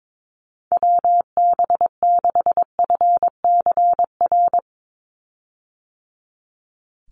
Morse code, heard as a keyed tone, about 22 wpm, 700 Hz. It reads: WB6FCR
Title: USA Call Signs 22wpm